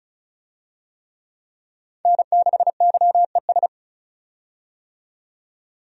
Morse code, heard as a keyed tone, about 36 wpm, 700 Hz. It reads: N6YES
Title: USA Call Signs - 5 Characters - 35wpm